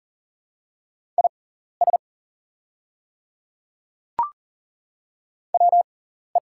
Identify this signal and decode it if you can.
Morse code, heard as a keyed tone, about 40 wpm, spaced as for 14 wpm, 700 Hz. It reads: IS WE
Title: 6x Character Spacing - Top 100 Words - 40wpm